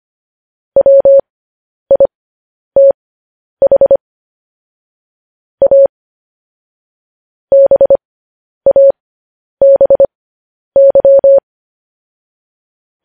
Morse code, heard as a keyed tone, about 25 wpm, spaced as for 10 wpm, 550 Hz. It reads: WITH A BABY
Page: news headlines in morse code at 10 wpm